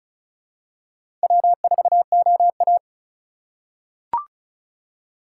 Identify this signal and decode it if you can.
Morse code, heard as a keyed tone, about 36 wpm, 700 Hz. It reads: W4OA
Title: USA Call Signs - 4 Characters - 2x Character Spacing - 35wpm